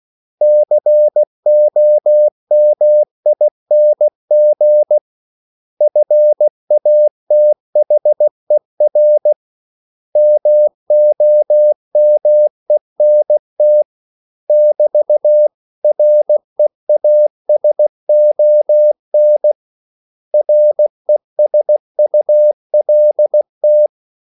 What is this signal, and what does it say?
Morse code, heard as a keyed tone, about 16 wpm, 600 Hz. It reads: COMING FATHER MOMENT <BT> REASON RESULT
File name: CW / Practice / 500MostCommonEnglishWords16